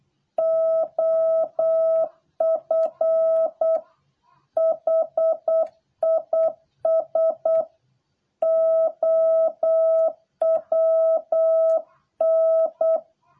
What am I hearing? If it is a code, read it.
Morse code, heard as a keyed tone, about 8 wpm, 650 Hz. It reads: OF HIS OWN